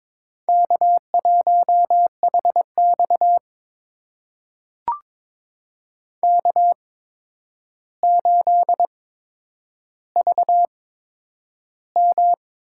Morse code, heard as a keyed tone, about 22 wpm, spaced as for 9 wpm, 700 Hz. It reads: K1HX K8VM
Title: USA Call Signs - 4 Characters - 8x Character Spacing - 22wpm